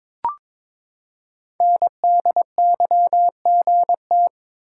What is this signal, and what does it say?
Morse code, heard as a keyed tone, about 22 wpm, 700 Hz. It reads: NDYGT
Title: Code Groups - 5 Letter:Numbers - 22wpm